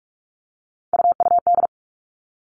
Morse code, heard as a keyed tone, about 44 wpm, 700 Hz. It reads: UUD
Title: All Cut Numbers - 3-Digit 45wpm